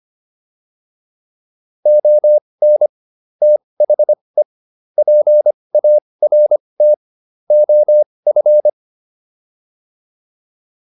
Morse code, heard as a keyed tone, about 25 wpm, spaced as for 20 wpm, 600 Hz. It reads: ON THE PART OF
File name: WD304 20